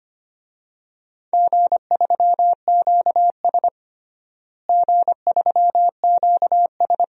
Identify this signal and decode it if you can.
Morse code, heard as a keyed tone, about 25 wpm, 700 Hz. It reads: G3QS G3QS
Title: DXCC-100 Call Signs - 4 Characters - Repeated 2x - 25wpm